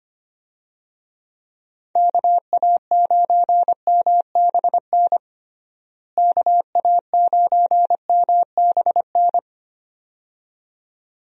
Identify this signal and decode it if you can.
Morse code, heard as a keyed tone, about 25 wpm, 700 Hz. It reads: KA9MBN KA9MBN
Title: USA Call Signs - 6 Characters - Repeated 2x - 25wpm